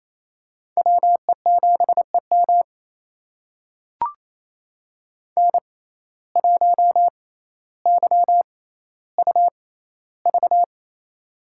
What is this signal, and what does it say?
Morse code, heard as a keyed tone, about 28 wpm, spaced as for 14 wpm, 700 Hz. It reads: WE7EM N1YUV
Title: USA Call Signs - 5 Characters - 6x Character Spacing - 28wpm